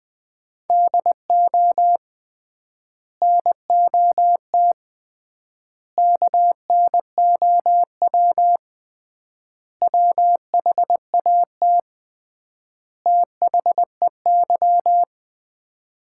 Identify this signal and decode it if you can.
Morse code, heard as a keyed tone, about 21 wpm, 700 Hz. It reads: DO NOT KNOW WHAT THEY
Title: 3x Word-Spacing - Sets of 5 Words - 20wpm